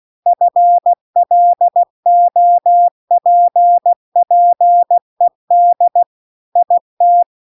Morse code, heard as a keyed tone, about 16 wpm, 700 Hz. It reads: FLOPPED IT